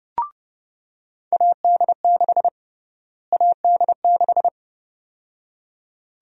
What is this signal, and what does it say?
Morse code, heard as a keyed tone, about 30 wpm, 700 Hz. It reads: AD6 AD6
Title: USA Call Sign Prefixes - Repeated 2x - 30wpm